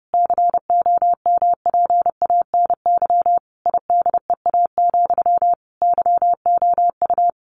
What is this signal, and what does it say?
Morse code, heard as a keyed tone, about 28 wpm, 700 Hz. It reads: COMPANY IDEA, YOU
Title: Sentences from Top 600 Words 30wpm